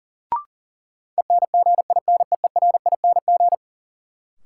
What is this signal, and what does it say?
Morse code, heard as a keyed tone, about 40 wpm, 700 Hz. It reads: ENGINEERING
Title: Sentences from Top 1500 Words - New Word Drill - 40wpm